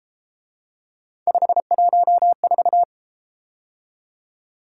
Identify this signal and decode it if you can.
Morse code, heard as a keyed tone, about 34 wpm, 700 Hz. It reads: 514